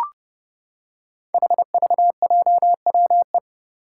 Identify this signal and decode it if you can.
Morse code, heard as a keyed tone, about 30 wpm, 700 Hz. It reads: HVJWE
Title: Five Challenging Letter-Number Combinations 30wpm